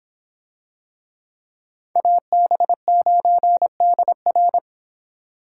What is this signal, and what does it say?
Morse code, heard as a keyed tone, about 25 wpm, 700 Hz. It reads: AB9DR